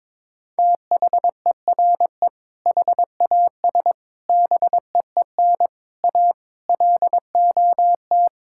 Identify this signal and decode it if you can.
Morse code, heard as a keyed tone, about 22 wpm, 700 Hz. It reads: THERE HAS BEEN A LOT
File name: practice / 22wpm / Sentences with 8-Word Phrases - 22wpm